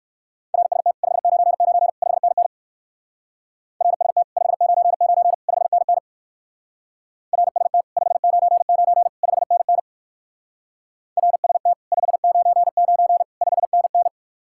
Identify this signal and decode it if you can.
Morse code, heard as a keyed tone, about 62 wpm, 700 Hz. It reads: RST 599 5NN RST 599 5NN RST 599 5NN RST 599 5NN